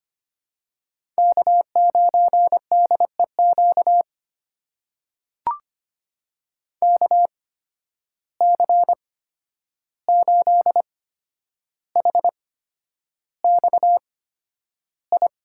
Morse code, heard as a keyed tone, about 25 wpm, spaced as for 10 wpm, 700 Hz. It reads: K9DEQ KC8HXI
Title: USA Call Signs - 8x Character Spacing - 25wpm